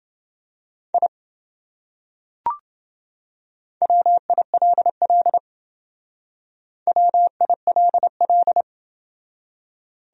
Morse code, heard as a keyed tone, about 26 wpm, 700 Hz. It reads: I WILL WILL